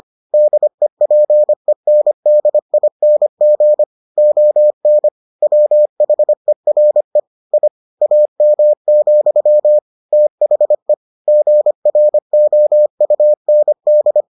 Morse code, heard as a keyed tone, about 25 wpm, 600 Hz. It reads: DEPENDING ON WHERE I AM, THE GROUND